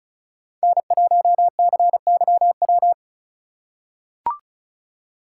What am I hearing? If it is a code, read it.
Morse code, heard as a keyed tone, about 34 wpm, 700 Hz. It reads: N1CYW
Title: USA Call Signs - 5 Characters - 4x Character Spacing - 35wpm